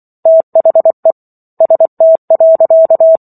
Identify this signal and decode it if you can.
Morse code, heard as a keyed tone, about 24 wpm, 650 Hz. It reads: THE ST.